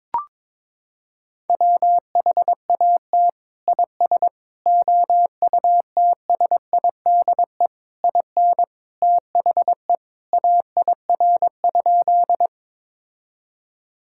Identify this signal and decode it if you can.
Morse code, heard as a keyed tone, about 22 wpm, 700 Hz. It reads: WHAT IS OUTSIDE IN THE AIR?
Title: Sentences from Top 1000 Words - Review Part 6 - 22wpm